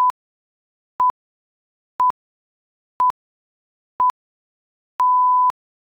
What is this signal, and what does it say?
Radio time signal (time pips). It is heard from 0.0 to 5.5 s.